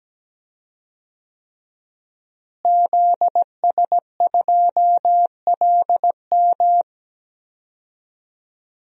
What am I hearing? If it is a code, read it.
Morse code, heard as a keyed tone, about 17 wpm, 700 Hz. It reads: ZS2LM
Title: DXCC-100 Call Signs - 5 Characters - Repeated 3x - 17wpm